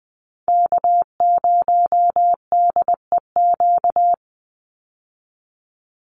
Morse code, heard as a keyed tone, about 20 wpm, 700 Hz. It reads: K0DEQ
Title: Call Signs Repeated 2x - 20wpm